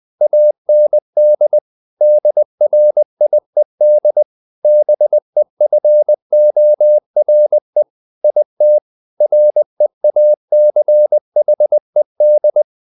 Morse code, heard as a keyed tone, about 20 wpm, 600 Hz. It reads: AND DRIED BEFORE IT REACHED
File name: the old man and the sea / the old man and the sea part 1 20wpm